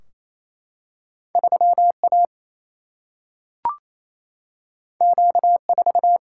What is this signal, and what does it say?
Morse code, heard as a keyed tone, about 28 wpm, 700 Hz. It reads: 3 A Q 4
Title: Single-Letter Single-Number 28wpm